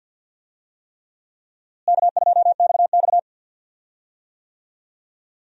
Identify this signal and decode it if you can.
Morse code, heard as a keyed tone, about 51 wpm, 700 Hz. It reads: K2XX
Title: USA Call Signs - 4 Characters - Repeated 3x - 50wpm